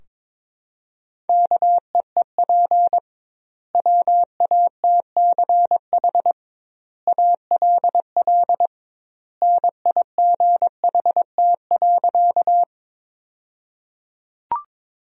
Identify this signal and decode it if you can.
Morse code, heard as a keyed tone, about 22 wpm, 700 Hz. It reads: KEEP WATCH ALL NIGHT.
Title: 2x Word-Spacing - Sentences from Top 300 Words - 22wpm